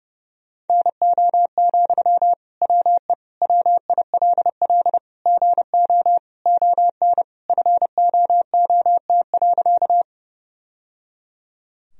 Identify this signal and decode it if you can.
Morse code, heard as a keyed tone, about 30 wpm, 700 Hz. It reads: NO, WE WILL GO ON FOOT.
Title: Sentences from Top 1000 Words - Review Part 3 - 30wpm